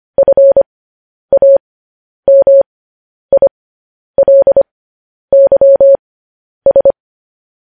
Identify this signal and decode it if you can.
Morse code, heard as a keyed tone, about 25 wpm, spaced as for 10 wpm, 550 Hz. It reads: FAMILYS
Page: news-headlines-in-morse-code-at-10-wpm